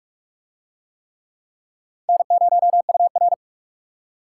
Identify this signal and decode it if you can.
Morse code, heard as a keyed tone, about 41 wpm, 700 Hz. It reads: N0UR